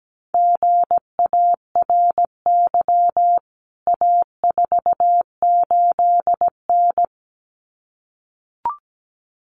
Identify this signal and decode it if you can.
Morse code, heard as a keyed tone, about 17 wpm, 700 Hz. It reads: GARY A48N
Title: CWT Exchange 17wpm